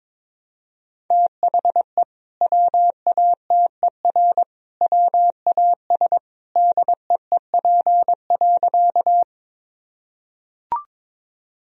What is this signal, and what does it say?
Morse code, heard as a keyed tone, about 22 wpm, 700 Hz. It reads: THE WATER WAS DEEP.
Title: Sentences from Top 1000 Words - Review Part 3 - 22wpm